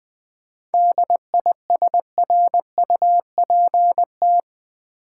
Morse code, heard as a keyed tone, about 20 wpm, 700 Hz. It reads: DISRUPT
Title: 7 Letter Words - First 3 Letters Given - 20wpm